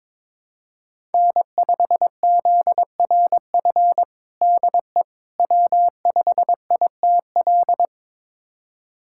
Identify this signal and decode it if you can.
Morse code, heard as a keyed tone, about 22 wpm, 700 Hz. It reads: N5ZRF DE W5ITL
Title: USA Callsign DE Callsign - 5 Characters - 22wpm